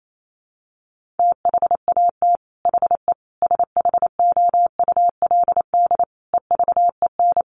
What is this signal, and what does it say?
Morse code, heard as a keyed tone, about 28 wpm, 700 Hz. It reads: THAT HE SHOULD EVEN
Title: Sentences from Top 1500 Words 28wpm